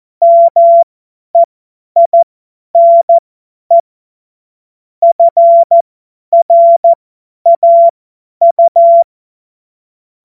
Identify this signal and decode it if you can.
Morse code, heard as a keyed tone, about 14 wpm, spaced as for 10 wpm, 680 Hz. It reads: MEINE FRAU
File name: Morsekurs / Witze / Witz 15. November 2022 50 F70